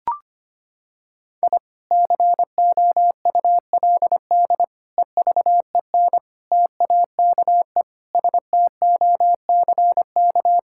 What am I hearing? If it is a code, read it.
Morse code, heard as a keyed tone, about 25 wpm, 700 Hz. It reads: I COULD EVEN TAKE STOCK
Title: Sentences from Top 1000 Words - Review Part 5 - 25wpm